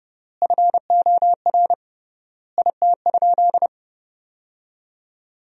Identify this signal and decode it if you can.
Morse code, heard as a keyed tone, about 30 wpm, 700 Hz. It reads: FOR IT?
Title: 3x Word-Spacing - Sentences from Top 400 Words - 30wpm